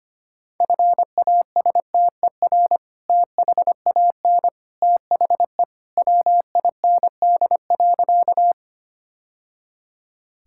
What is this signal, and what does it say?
Morse code, heard as a keyed tone, about 25 wpm, 700 Hz. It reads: FASTER THAN THE WIND.
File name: practice / 25wpm / Sentences with 2-Word Phrases - 25wpm